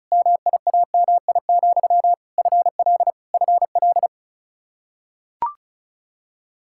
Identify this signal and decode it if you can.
Morse code, heard as a keyed tone, about 36 wpm, 700 Hz. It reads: MIAMI, FL FL
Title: K1USN SST QSO Practice 35wpm